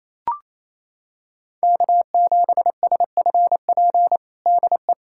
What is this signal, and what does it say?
Morse code, heard as a keyed tone, about 28 wpm, 700 Hz. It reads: K7SFP DE
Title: USA Callsign DE Callsign 28wpm